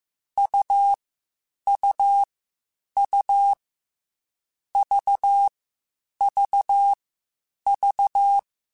Morse code, heard as a keyed tone, about 15 wpm, 800 Hz. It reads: U U U V V V
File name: Short Runs / Short Runs-mp3s / AUV AUV4 @13